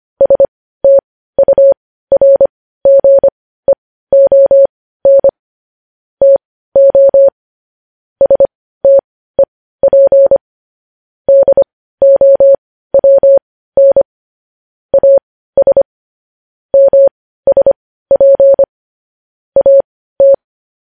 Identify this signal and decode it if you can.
Morse code, heard as a keyed tone, about 25 wpm, spaced as for 15 wpm, 550 Hz. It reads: STURGEON TO STEP DOWN AS MSP AT
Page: news headlines in morse code at 15 wpm